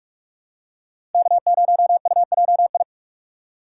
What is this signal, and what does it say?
Morse code, heard as a keyed tone, about 43 wpm, 700 Hz. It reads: K0UJI
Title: USA Call Signs - 5 Characters - 4x Character Spacing - 45wpm